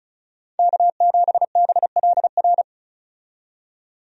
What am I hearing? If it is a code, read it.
Morse code, heard as a keyed tone, about 36 wpm, 700 Hz. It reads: K7BLR
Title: USA Call Signs - 5 Characters - 35wpm